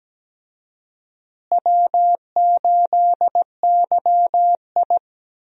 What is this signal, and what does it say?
Morse code, heard as a keyed tone, about 17 wpm, 700 Hz. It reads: W8YI
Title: USA Call Signs - 4 Characters - Repeated 3x - 17wpm